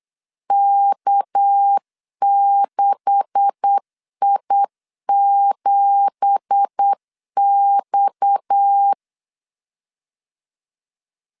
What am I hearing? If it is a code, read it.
Morse code, heard as a keyed tone, about 8 wpm, 800 Hz. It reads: K6I7X
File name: CW / DET / 1 9 1500C S B 8wpm 5624